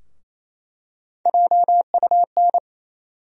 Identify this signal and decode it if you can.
Morse code, heard as a keyed tone, about 28 wpm, 700 Hz. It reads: JUN
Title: USA Call Sign Postfixes - 3 Letters 28wpm